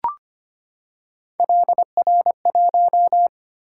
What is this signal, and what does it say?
Morse code, heard as a keyed tone, about 25 wpm, 700 Hz. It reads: LR1
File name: Three Letter-Number Combinations - 25wpm